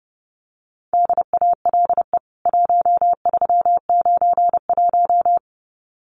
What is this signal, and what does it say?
Morse code, heard as a keyed tone, about 30 wpm, 700 Hz. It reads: DALE 1391
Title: CWT Exchange 30wpm